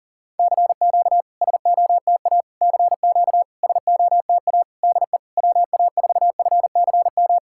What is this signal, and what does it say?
Morse code, heard as a keyed tone, about 44 wpm, 700 Hz. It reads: CQ SOTA CQ SOTA DE WA4FCM
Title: SOTA QSO Practice 40wpm